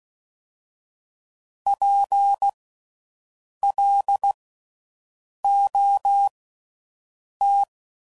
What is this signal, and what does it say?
Morse code, heard as a keyed tone, about 16 wpm, spaced as for 6 wpm, 800 Hz. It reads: PLOT
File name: Suffixes / Special INGs / Slow With Faster Ings